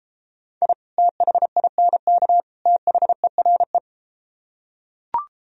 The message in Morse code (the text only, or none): I THINK THERE